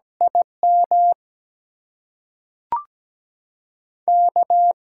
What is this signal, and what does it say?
Morse code, heard as a keyed tone, about 17 wpm, 700 Hz. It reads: IM K
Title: USA Call Signs - 6 Characters - 8x Character Spacing - 17wpm